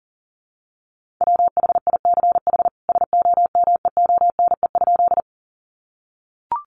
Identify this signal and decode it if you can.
Morse code, heard as a keyed tone, about 37 wpm, 700 Hz. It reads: WHICH SOMEONE?
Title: Sentences from Top 600 Words 40wpm